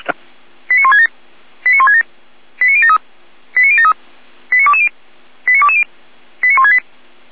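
Five-tone selective call (ZVEI), heard as quick runs of five tones, tones of about 70 ms. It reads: 89167 89167 89073 89073 89220 89220 89168
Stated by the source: ZVEI five-tone selective call received on an SDR.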